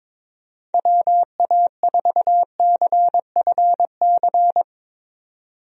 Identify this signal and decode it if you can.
Morse code, heard as a keyed tone, about 22 wpm, 700 Hz. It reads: WA4CFC
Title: USA Call Signs - 6 Characters - 22wpm